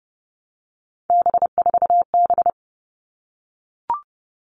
Morse code, heard as a keyed tone, about 30 wpm, 700 Hz. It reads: B4B
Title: All Cut Numbers - 3-Digit 30wpm